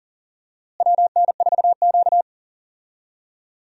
Morse code, heard as a keyed tone, about 40 wpm, 700 Hz. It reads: WN4Q
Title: USA Call Signs - 4 Characters - Repeated 3x - 40wpm